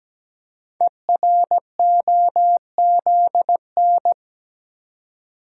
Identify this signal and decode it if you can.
Morse code, heard as a keyed tone, about 17 wpm, 700 Hz. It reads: EROZN
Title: Code Groups - 5 Letters - 17wpm